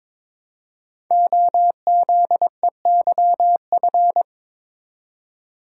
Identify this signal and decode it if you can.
Morse code, heard as a keyed tone, about 22 wpm, 700 Hz. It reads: OZEYF
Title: Code Groups - 5 Letter:Numbers - 22wpm